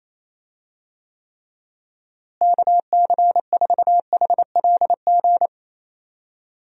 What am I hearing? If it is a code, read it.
Morse code, heard as a keyed tone, about 27 wpm, 700 Hz. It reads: KC4HLG
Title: USA Call Signs - 6 Characters - Repeated 3x - 28wpm